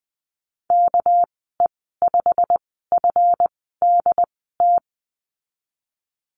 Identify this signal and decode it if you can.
Morse code, heard as a keyed tone, about 20 wpm, spaced as for 15 wpm, 700 Hz. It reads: KE5FDT
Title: 213 Lesson 34g - Number 8 - Call Signs 20wpm